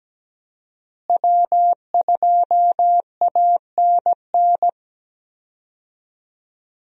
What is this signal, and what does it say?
Morse code, heard as a keyed tone, about 17 wpm, 700 Hz. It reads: W2ANN
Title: USA Call Signs - 5 Characters - Repeated 3x - 17wpm